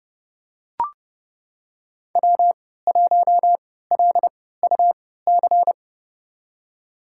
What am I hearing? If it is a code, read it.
Morse code, heard as a keyed tone, about 30 wpm, spaced as for 17 wpm, 700 Hz. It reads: W1LUC
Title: USA Call Signs - 3x Character Spacing - 30wpm